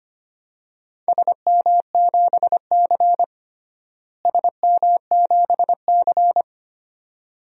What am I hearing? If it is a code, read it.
Morse code, heard as a keyed tone, about 25 wpm, 700 Hz. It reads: SM7C SM7C
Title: DXCC-100 Call Signs - 4 Characters - Repeated 3x - 25wpm